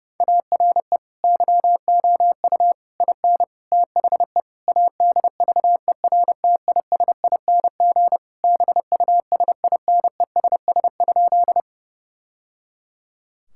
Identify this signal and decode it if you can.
Morse code, heard as a keyed tone, about 30 wpm, 700 Hz. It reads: ARE YOU IN THE ADVERTISING BUSINESS?
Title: Sentences from Top 1000 Words - Review Part 6 - 30wpm